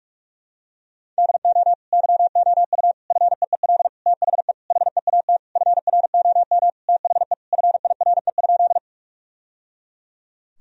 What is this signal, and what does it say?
Morse code, heard as a keyed tone, about 46 wpm, 700 Hz. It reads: DO YOU FEEL THE HEAT FROM THE FIRE?
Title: Sentences from Top 1000 Words - Review Part 3 - 45wpm